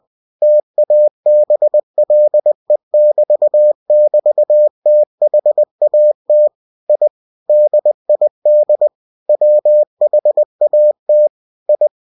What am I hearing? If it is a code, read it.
Morse code, heard as a keyed tone, about 20 wpm, 600 Hz. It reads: TABLE<BT><BT>THAT I DID WHAT I
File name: cw-books / winnie-the-pooh / winnie-20wpm-page-0107